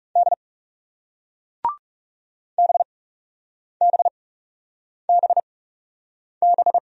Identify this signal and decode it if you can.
Morse code, heard as a keyed tone, about 40 wpm, spaced as for 8 wpm, 700 Hz. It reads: D BBBB